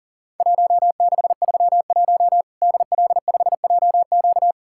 Morse code, heard as a keyed tone, about 40 wpm, 700 Hz. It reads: 1631 DL5JQ
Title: Worked All Europe DX Contest - QTC Exchange - 40wpm